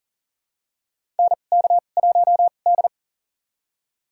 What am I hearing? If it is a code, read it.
Morse code, heard as a keyed tone, about 40 wpm, spaced as for 29 wpm, 700 Hz. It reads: NK1D
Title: USA Call Signs - 4 Characters - 2x Character Spacing - 40wpm